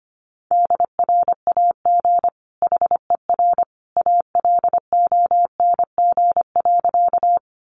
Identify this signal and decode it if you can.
Morse code, heard as a keyed tone, about 25 wpm, 700 Hz. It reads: DRAG HER ALONG.
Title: Sentences from Top 1500 Words 25wpm